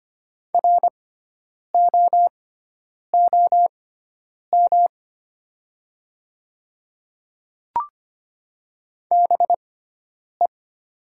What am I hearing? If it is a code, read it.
Morse code, heard as a keyed tone, about 25 wpm, spaced as for 9 wpm, 700 Hz. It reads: ROOM BE
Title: 6x Character Spacing - Top 300 Words - 25wpm